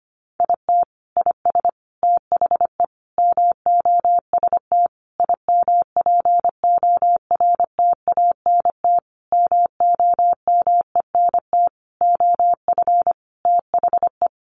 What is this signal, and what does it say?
Morse code, heard as a keyed tone, about 25 wpm, 700 Hz. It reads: IT IS THE MOST IMPORTANT MOMENT OF THE
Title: Sentences from Top 1500 Words 25wpm